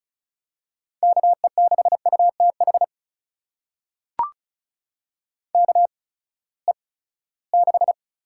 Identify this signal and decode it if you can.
Morse code, heard as a keyed tone, about 34 wpm, spaced as for 20 wpm, 700 Hz. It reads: KE6UTH KE6
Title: USA Call Signs - 6 Characters - 8x Character Spacing - 35wpm